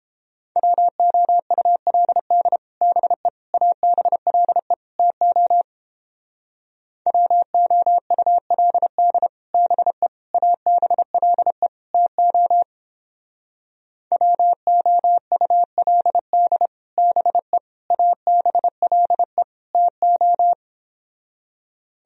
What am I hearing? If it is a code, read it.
Morse code, heard as a keyed tone, about 31 wpm, 700 Hz. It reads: WOULD BE ABLE TO WOULD BE ABLE TO WOULD BE ABLE TO